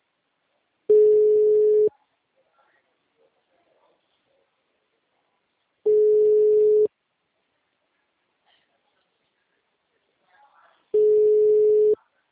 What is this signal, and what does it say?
Ringback tone (Europe, 425 Hz), with speech.